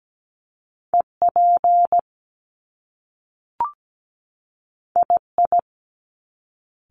Morse code, heard as a keyed tone, about 17 wpm, 700 Hz. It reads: EP II
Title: Top Two Letter Combinations in Words 17wpm